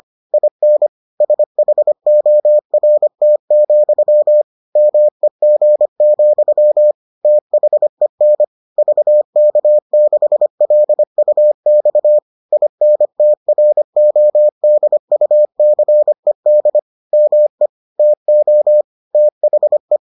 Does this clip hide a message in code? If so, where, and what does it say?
Yes: Morse code from 0.3 to 20.0 s, heard as a keyed tone, about 25 wpm, 600 Hz. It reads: IN SHORT, MEG, THEN VK6LUX INTRODUCED ME TO THE